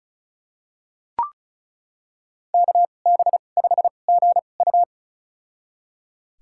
Morse code, heard as a keyed tone, about 36 wpm, spaced as for 25 wpm, 700 Hz. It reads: KB5GU
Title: USA Call Signs - 5 Characters - 2x Character Spacing - 35wpm